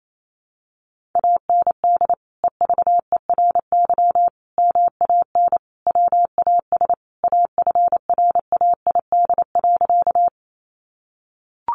Morse code, heard as a keyed tone, about 28 wpm, 700 Hz. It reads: AND EVERY MAN WAS AFRAID.
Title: Sentences from Top 1500 Words 28wpm